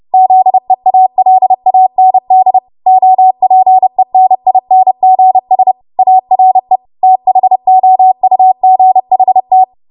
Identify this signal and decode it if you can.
Morse code, heard as a keyed tone, about 30 wpm, 750 Hz. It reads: ZEALAND OPENINGS ARE THOUGHT
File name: Morse / 30WPM